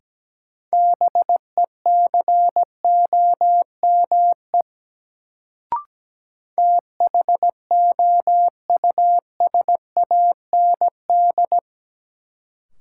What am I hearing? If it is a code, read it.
Morse code, heard as a keyed tone, about 17 wpm, 700 Hz. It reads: BECOME THOUSAND
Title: Sentences from Top 600 Words - New Word Drill - 17wpm